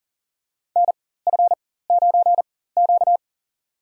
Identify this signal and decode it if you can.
Morse code, heard as a keyed tone, about 40 wpm, spaced as for 19 wpm, 700 Hz. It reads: NF9Q
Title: USA Call Signs - 4 Characters - 4x Character Spacing - 40wpm